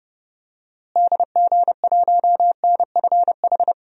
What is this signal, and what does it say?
Morse code, heard as a keyed tone, about 30 wpm, 700 Hz. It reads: DG1NFH
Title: DXCC-100 Call Signs - 6 Characters - Repeated 3x - 30wpm